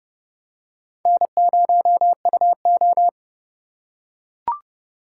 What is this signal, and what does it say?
Morse code, heard as a keyed tone, about 29 wpm, 700 Hz. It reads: N0UO